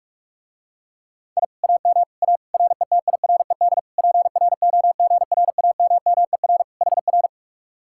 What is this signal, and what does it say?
Morse code, heard as a keyed tone, about 48 wpm, 700 Hz. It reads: I AM A RETIRED PROGRAMMER HR